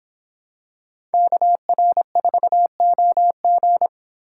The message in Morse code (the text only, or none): KR4OG